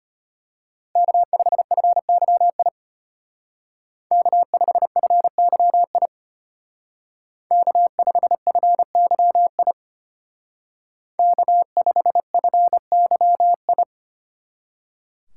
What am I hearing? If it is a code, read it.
Morse code, heard as a keyed tone, about 30 wpm, 700 Hz. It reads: K5FYI K5FYI K5FYI K5FYI
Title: Speed Racing - USA 1x3 Call Signs 25wpm